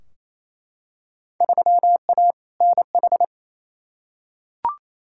Morse code, heard as a keyed tone, about 28 wpm, 700 Hz. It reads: 3A NH
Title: Field Day Exchanges Encore - 28wpm